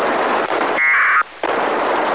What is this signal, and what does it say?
Packet radio (AX.25, as in APRS).
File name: ho-68-12-24-2009